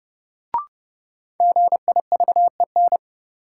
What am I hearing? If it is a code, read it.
Morse code, heard as a keyed tone, about 31 wpm, 700 Hz. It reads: GIVEN
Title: Sentences from Top 1400 Words - New Word Drill Alt - 30wpm